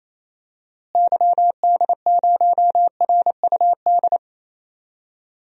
Morse code, heard as a keyed tone, about 28 wpm, 700 Hz. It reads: YD0RUD
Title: DXCC-100 Call Signs - 6 Characters - Repeated 2x - 28wpm